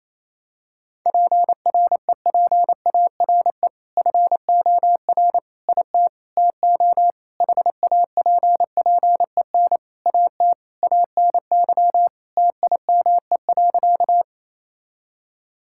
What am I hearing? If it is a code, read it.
Morse code, heard as a keyed tone, about 28 wpm, 700 Hz. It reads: PREPARE FOR IT TO HAPPEN AT ANY TIME.
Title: Sentences from Top 1000 Words - Review Part 6 - 28wpm